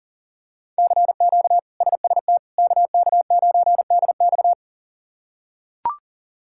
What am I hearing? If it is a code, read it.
Morse code, heard as a keyed tone, about 40 wpm, 700 Hz. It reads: CQ SST KK9DX
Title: K1USN SST QSO Practice - 40wpm